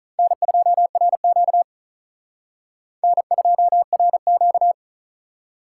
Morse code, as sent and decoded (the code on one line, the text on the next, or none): -. ..--- .-. --.- / -. ..--- .-. --.-
N2RQ N2RQ